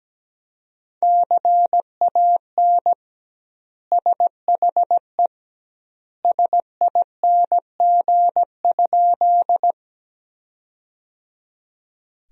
Morse code, heard as a keyed tone, about 17 wpm, 700 Hz. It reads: CAN SHE SING?
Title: 2x Word-Spacing - Sentences from Top 400 Words - 17wpm